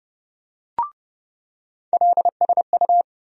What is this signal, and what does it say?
Morse code, heard as a keyed tone, about 30 wpm, 700 Hz. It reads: LSU
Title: USA Call Sign Postfixes - 3 Letters 30wpm